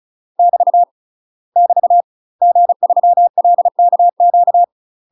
Morse code, heard as a keyed tone, about 34 wpm, 700 Hz. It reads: <BT> <BT> G3LKQ
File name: UK MASTER QSOs TRAINING / 35WPM / UK MASTER QSOs 0021